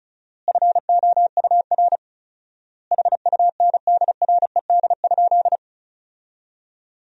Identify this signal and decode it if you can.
Morse code, heard as a keyed tone, about 36 wpm, 700 Hz. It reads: FOUR HUNDRED?
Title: 4x Word-Spacing - Sentences from Top 400 Words - 35wpm